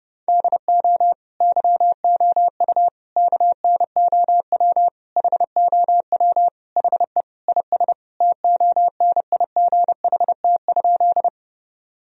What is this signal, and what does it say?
Morse code, heard as a keyed tone, about 32 wpm, 700 Hz. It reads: DO YOU KNOW HOW HE IS TONIGHT?
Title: Sentences from Top 1000 Words - Review Part 3 - 30wpm